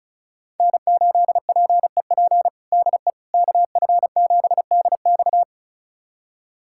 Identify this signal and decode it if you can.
Morse code, heard as a keyed tone, about 36 wpm, 700 Hz. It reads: N8PEP DE KF7DX
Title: USA Callsign DE Callsign - 5 Characters - 35wpm